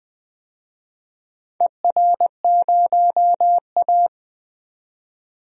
Morse code, heard as a keyed tone, about 20 wpm, 700 Hz. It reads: ER0A
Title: DXCC-100 Call Signs - 4 Characters - 20wpm